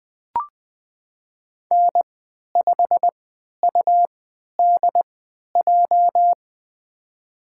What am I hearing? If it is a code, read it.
Morse code, heard as a keyed tone, about 20 wpm, spaced as for 11 wpm, 700 Hz. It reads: N5UDJ
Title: USA Call Signs - 5 Characters - 3x Character Spacing - 20wpm